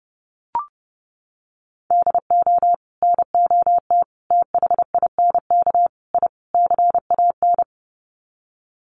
Morse code, heard as a keyed tone, about 30 wpm, 700 Hz. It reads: DO NOT THINK I CAN
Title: Sets of 5 Words Encore2 30wpm